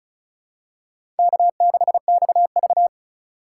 Morse code, heard as a keyed tone, about 34 wpm, 700 Hz. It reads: K6XV
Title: USA Call Signs - 4 Characters - 4x Character Spacing - 35wpm